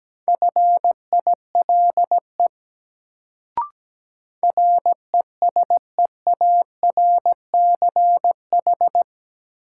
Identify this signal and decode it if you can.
Morse code, heard as a keyed tone, about 17 wpm, 700 Hz. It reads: FILE RESEARCH
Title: Sentences from Top 1000 Words - Review Part 4 - New Word Drill - 17wpm